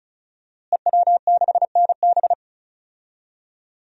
Morse code, heard as a keyed tone, about 36 wpm, 700 Hz. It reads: EW6NB
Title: DXCC-100 Call Signs - 5 Characters - Repeated 3x - 35wpm